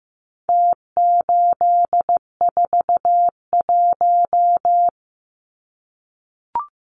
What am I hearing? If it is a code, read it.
Morse code, heard as a keyed tone, about 15 wpm, 700 Hz. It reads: T841
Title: Numbers - Contest Style - 4-Digit 15wpm